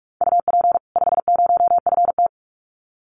Morse code, heard as a keyed tone, about 44 wpm, 700 Hz. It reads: UP 50FT
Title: Common QSO Phrases 45wpm